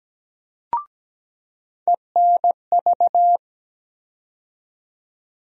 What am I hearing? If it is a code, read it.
Morse code, heard as a keyed tone, about 17 wpm, 700 Hz. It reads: ENV
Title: USA Call Sign Postfixes - 3 Letters 17wpm